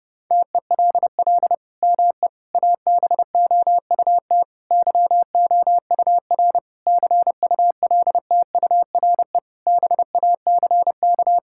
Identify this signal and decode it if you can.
Morse code, heard as a keyed tone, about 30 wpm, 700 Hz. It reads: TELL ME ABOUT YOUR CULTURE BACK
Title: Sentences from Top 1000 Words - Review Part 6 - 30wpm